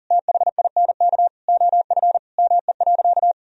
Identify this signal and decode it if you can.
Morse code, heard as a keyed tone, about 40 wpm, 700 Hz. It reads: THINK OF ME.
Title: Sentences from Top 1000 Words - Review Part 1 - 40wpm